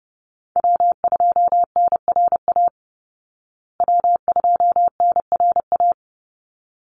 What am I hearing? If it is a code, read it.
Morse code, heard as a keyed tone, about 30 wpm, 700 Hz. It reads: W2NRA W2NRA
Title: Call Signs Repeated 3x - 30wpm